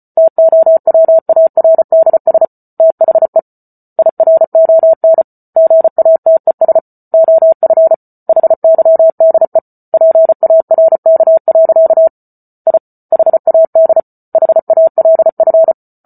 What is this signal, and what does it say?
Morse code, heard as a keyed tone, about 34 wpm, 650 Hz. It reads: TOWARDS THE IRON GATES OF HYDE PARK. I HAD HALF